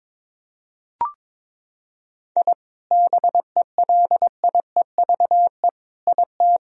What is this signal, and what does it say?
Morse code, heard as a keyed tone, about 22 wpm, 700 Hz. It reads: I BELIEVE IT
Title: Sentences from Top 1000 Words - Review Part 2 - 22wpm